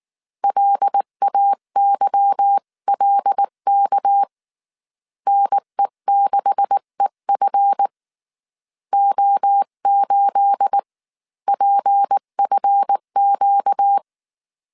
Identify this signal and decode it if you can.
Morse code, heard as a keyed tone, about 19 wpm, spaced as for 16 wpm, 800 Hz. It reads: LAYLK NE6EF O8PFQ
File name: CW / DET / 1 9 1500C S B 18wpm 2551